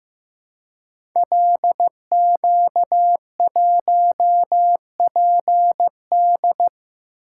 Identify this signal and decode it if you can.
Morse code, heard as a keyed tone, about 15 wpm, 700 Hz. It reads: LQ1PD
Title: Code Groups - 5 Letter:Numbers - 15wpm